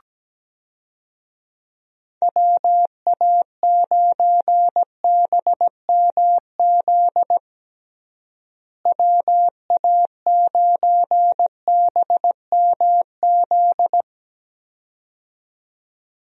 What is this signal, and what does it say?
Morse code, heard as a keyed tone, about 17 wpm, 700 Hz. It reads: WA9BMZ WA9BMZ
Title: USA Call Signs - 6 Characters - Repeated 2x - 17wpm